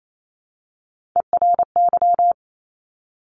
Morse code, heard as a keyed tone, about 28 wpm, 700 Hz. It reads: ERY
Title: Top Three Letter Combinations in Words 28wpm